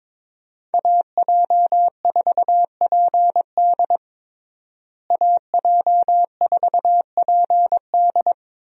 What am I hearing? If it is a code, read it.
Morse code, heard as a keyed tone, about 22 wpm, 700 Hz. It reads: AJ4PD AJ4PD
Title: USA Call Signs - 5 Characters - Repeated 2x - 22wpm